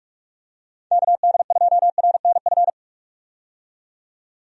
Morse code, heard as a keyed tone, about 44 wpm, 700 Hz. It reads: KD2RNF
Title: USA Call Signs - 6 Characters - 45wpm